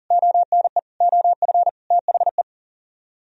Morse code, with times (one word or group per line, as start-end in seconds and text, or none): 0.1-0.8 ONE
1.0-1.7 OF
1.9-2.4 THE